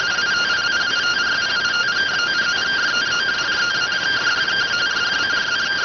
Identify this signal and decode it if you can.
Two-tone teleprinter signal, about 170 Hz shift.